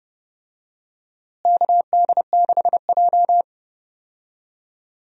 Morse code, heard as a keyed tone, about 32 wpm, 700 Hz. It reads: KD6J
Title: USA Call Signs - 4 Characters - Repeated 3x - 30wpm